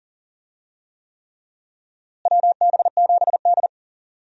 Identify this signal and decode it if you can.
Morse code, heard as a keyed tone, about 37 wpm, 700 Hz. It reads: WB7D
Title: USA Call Signs - 4 Characters - Repeated 3x - 40wpm